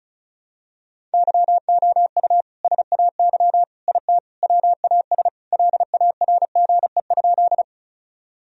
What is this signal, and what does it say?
Morse code, heard as a keyed tone, about 34 wpm, 700 Hz. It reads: YOU SAY IT WAS LARGE?
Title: Sentences from Top 1000 Words - Review Part 1 - 35wpm